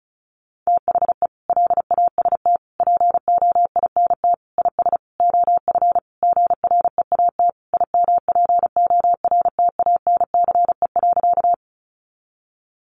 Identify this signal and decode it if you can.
Morse code, heard as a keyed tone, about 34 wpm, 700 Hz. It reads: THE LAST POINT IS OF GREAT IMPORTANCE.
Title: Sentences from Top 1200 Words 35wpm